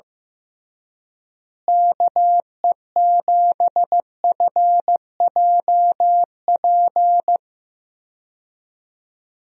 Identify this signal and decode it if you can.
Morse code, heard as a keyed tone, about 15 wpm, 700 Hz. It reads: KE7FJP
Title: USA Call Signs - 6 Characters - Repeated 3x - 15wpm